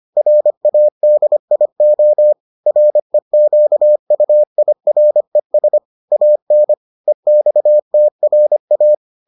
Morse code, heard as a keyed tone, about 25 wpm, 600 Hz. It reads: RADIO REQUIRES AN EXTRA